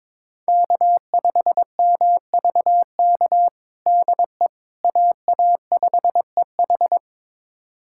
Morse code, heard as a keyed tone, about 22 wpm, 700 Hz. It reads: K5MVK DE AA5EH
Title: USA Callsign DE Callsign - 5 Characters - 22wpm